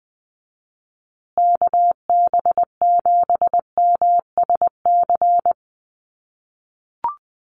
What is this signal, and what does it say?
Morse code, heard as a keyed tone, about 20 wpm, 700 Hz. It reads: KB7MSC
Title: USA Call Signs Repeated 2x 20wpm